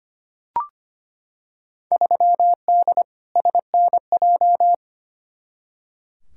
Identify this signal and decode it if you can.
Morse code, heard as a keyed tone, about 25 wpm, 700 Hz. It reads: 3D SNJ
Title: Field Day Exchanges Encore - 25wpm